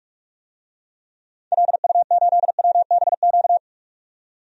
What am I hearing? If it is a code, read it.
Morse code, heard as a keyed tone, about 45 wpm, 700 Hz. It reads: LU8WBQ
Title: DXCC-100 Call Signs - 6 Characters - 45wpm